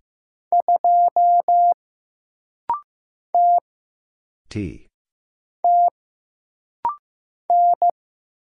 Morse code, heard as a keyed tone, about 15 wpm, 700 Hz. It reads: 2 T T N
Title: Single Letter-Number Alt - Rapid-Fire 15wpm